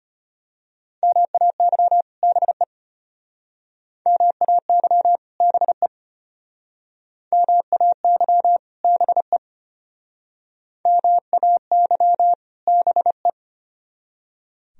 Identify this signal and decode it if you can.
Morse code, heard as a keyed tone, about 31 wpm, 700 Hz. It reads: MAY BE MAY BE MAY BE MAY BE
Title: Speed-Racing - Sets of 2 Words 25wpm